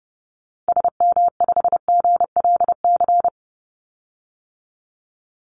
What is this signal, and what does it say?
Morse code, heard as a keyed tone, about 30 wpm, 700 Hz. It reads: SM5GLC
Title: Call Signs - 30wpm